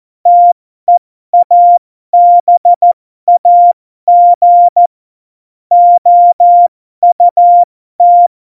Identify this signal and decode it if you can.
Morse code, heard as a keyed tone, about 14 wpm, spaced as for 12 wpm, 700 Hz. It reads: TEABAG OUT
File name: Morsekurs / Witze